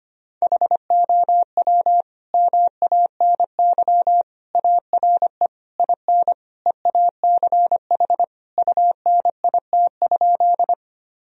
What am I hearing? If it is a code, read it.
Morse code, heard as a keyed tone, about 25 wpm, 700 Hz. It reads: HOW MANY ARE IN EACH UNIT?
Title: Sentences from Top 1000 Words - Review Part 2 - 25wpm